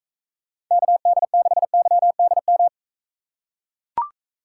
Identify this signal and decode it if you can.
Morse code, heard as a keyed tone, about 41 wpm, 700 Hz. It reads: KD6YDM